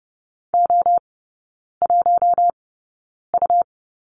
Morse code, heard as a keyed tone, about 30 wpm, spaced as for 9 wpm, 700 Hz. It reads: O1U
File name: Lesson 19e - Letter w - Three Characters 30wpm